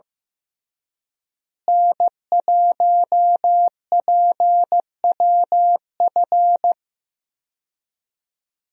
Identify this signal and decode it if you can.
Morse code, heard as a keyed tone, about 15 wpm, 700 Hz. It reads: N1PWF
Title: USA Call Signs - 5 Characters - Repeated 3x - 15wpm